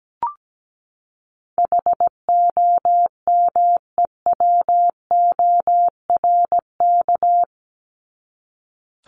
Morse code, heard as a keyed tone, about 17 wpm, 700 Hz. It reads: HOMEWORK
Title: Top 1501-1750 Words 17wpm